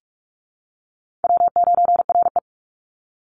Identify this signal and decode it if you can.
Morse code, heard as a keyed tone, about 44 wpm, 700 Hz. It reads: W9RE
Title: Easy Call Signs Repeated 2x - 45wpm